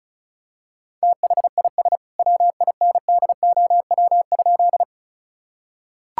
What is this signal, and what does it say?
Morse code, heard as a keyed tone, about 36 wpm, 700 Hz. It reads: THIS WINDOW?
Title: Sentences from Top 1600 Words 35wpm